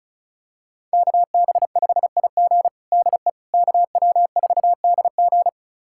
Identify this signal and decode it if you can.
Morse code, heard as a keyed tone, about 34 wpm, 700 Hz. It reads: KB5IG DE KW4DG
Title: USA Callsign DE Callsign - 5 Characters - 35wpm